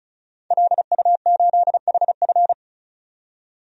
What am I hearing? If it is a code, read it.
Morse code, heard as a keyed tone, about 34 wpm, 700 Hz. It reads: LU8HF
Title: DXCC-100 Call Signs - 5 Characters - 35wpm